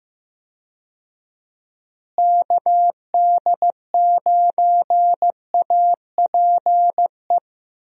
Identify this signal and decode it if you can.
Morse code, heard as a keyed tone, about 15 wpm, 700 Hz. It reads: KD9APE
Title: USA Call Signs - 6 Characters - Repeated 2x - 15wpm